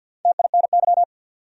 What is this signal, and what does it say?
Morse code, heard as a keyed tone, about 47 wpm, 700 Hz. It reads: TINY
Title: Four Letter Words - 50wpm